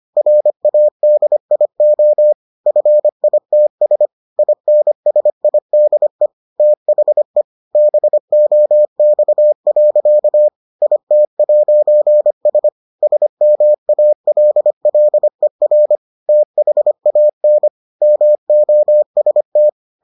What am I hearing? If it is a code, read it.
Morse code, heard as a keyed tone, about 25 wpm, 600 Hz. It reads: RADIO FITS INSIDE THE BOX. IT'S SMALLER THAN MOST